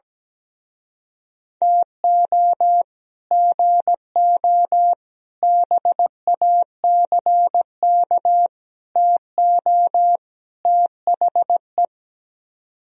Morse code, heard as a keyed tone, about 17 wpm, 700 Hz. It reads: TO GO BACK TO THE
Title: Speed-Racing - Sets of 5 Words 17wpm